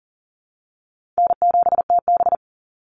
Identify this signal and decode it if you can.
Morse code, heard as a keyed tone, about 40 wpm, 700 Hz. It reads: N7TB
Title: Call Signs Repeated 2x - 40wpm